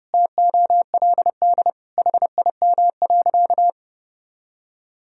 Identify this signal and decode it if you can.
Morse code, heard as a keyed tone, about 32 wpm, 700 Hz. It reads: TOLD HIM.
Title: Sentences from Top 1000 Words - Review Part 1 - 30wpm